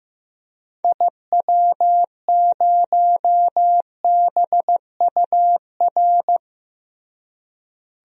Morse code, heard as a keyed tone, about 14 wpm, 700 Hz. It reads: IW0BUR
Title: DXCC-100 Call Signs - 6 Characters - Repeated 3x - 15wpm